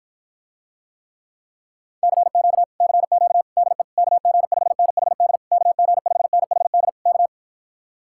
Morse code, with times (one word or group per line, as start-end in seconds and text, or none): 2.0-2.6 CQ
2.8-3.4 CQ
3.6-3.8 DE
4.0-5.4 KG5NHD
5.5-6.9 KG5NHD
7.1-7.3 K